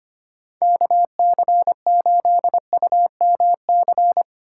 Morse code, heard as a keyed tone, about 25 wpm, 700 Hz. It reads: KC8UMC
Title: USA Call Signs - 6 Characters - 3x Character Spacing - 25wpm